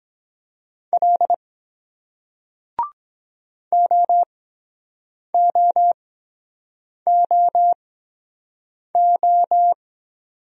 Morse code, heard as a keyed tone, about 23 wpm, 700 Hz. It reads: L O O O O